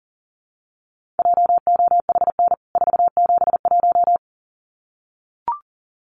Morse code, heard as a keyed tone, about 40 wpm, 700 Hz. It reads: JOHN 471
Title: CWT Exchange 40wpm